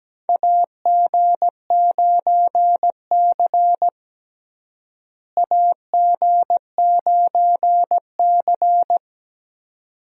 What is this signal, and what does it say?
Morse code, heard as a keyed tone, about 17 wpm, 700 Hz. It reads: AG9C AG9C
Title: USA Call Signs - 4 Characters - Repeated 2x - 17wpm